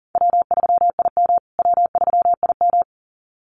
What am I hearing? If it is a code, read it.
Morse code, heard as a keyed tone, about 40 wpm, 700 Hz. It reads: W3IM W3IM
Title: USA Call Signs Repeated 2x 40wpm